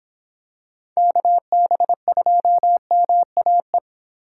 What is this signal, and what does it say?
Morse code, heard as a keyed tone, about 26 wpm, 700 Hz. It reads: KB2MAE